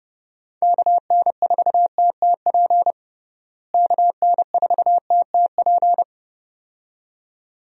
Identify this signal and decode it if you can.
Morse code, heard as a keyed tone, about 30 wpm, 700 Hz. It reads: KN4TTP KN4TTP
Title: USA Call Signs - 6 Characters - Repeated 2x - 30wpm